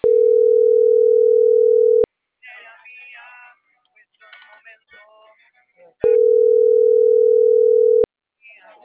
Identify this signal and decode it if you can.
Ringback tone (North America), 440 + 480 Hz.